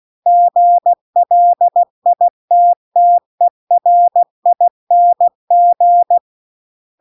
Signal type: Morse code